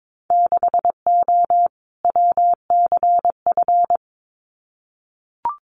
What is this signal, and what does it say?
Morse code, heard as a keyed tone, about 22 wpm, 700 Hz. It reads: 6O WCF
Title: Winter Field Day Exchanges Encore 22wpm